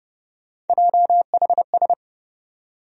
Morse code, heard as a keyed tone, about 30 wpm, 700 Hz. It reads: JHS
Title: USA Call Sign Postfixes - 3 Letters 30wpm